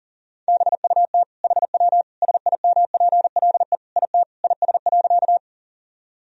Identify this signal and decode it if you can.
Morse code, heard as a keyed tone, about 37 wpm, 700 Hz. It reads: BUT HW SIMPLE IT IS.
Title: Sentences from Top 400 Words and Abbreviations 40wpm